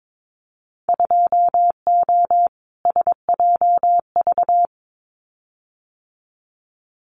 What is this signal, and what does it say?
Morse code, heard as a keyed tone, about 22 wpm, 700 Hz. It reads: 2O SJV
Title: Winter Field Day Exchanges Encore 22wpm